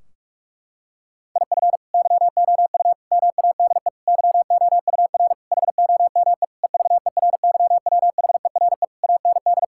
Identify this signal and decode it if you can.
Morse code, heard as a keyed tone, about 43 wpm, 700 Hz. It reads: IF YOU MADE YOUR HOME EVERYWHERE AND
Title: Sentences from Top 1600 Words 45wpm